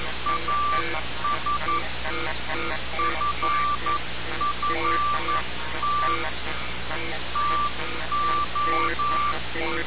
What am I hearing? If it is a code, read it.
Morse code, heard as a keyed tone, about 11 wpm, 1170 Hz. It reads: AS FFT TO
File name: radiocap / REC z